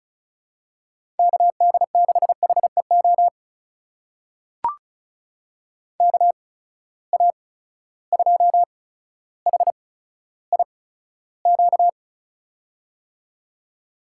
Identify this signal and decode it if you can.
Morse code, heard as a keyed tone, about 36 wpm, spaced as for 15 wpm, 700 Hz. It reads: KD6HEO KA2HIQ
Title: USA Call Signs - 6 Characters - 8x Character Spacing - 35wpm